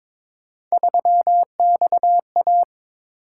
Morse code, heard as a keyed tone, about 22 wpm, 700 Hz. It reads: 3XA
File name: practice / 22wpm / Three Letter-Number Combinations - 22wpm